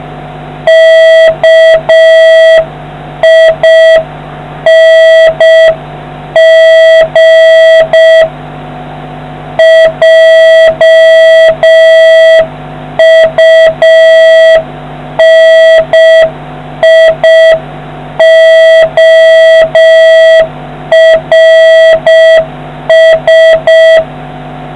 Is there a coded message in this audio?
Yes: Morse code from 0.7 to 24.0 s, heard as a keyed tone, about 6 wpm, 650 Hz, with noise. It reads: KING JUNIORS